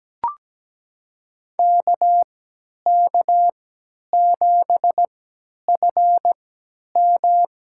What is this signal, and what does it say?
Morse code, heard as a keyed tone, about 17 wpm, spaced as for 10 wpm, 700 Hz. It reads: KK7FM
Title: USA Call Signs - 5 Characters - 3x Character Spacing - 17wpm